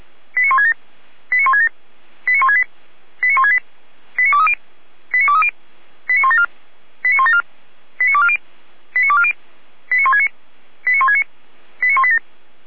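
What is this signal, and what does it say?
Five-tone selective call (ZVEI), heard as quick runs of five tones, tones of about 70 ms. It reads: 89167 89167 89168 89168 89230 89230 89174 89174 89240 89240 89169 89169 89187
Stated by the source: ZVEI five-tone call 89121 received on SDR.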